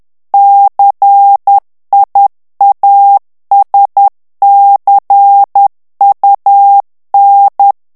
Morse code, heard as a keyed tone, about 10 wpm, 800 Hz. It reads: CIASCUN
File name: ITALIEN texte clair 2 vitesse 10 mots minute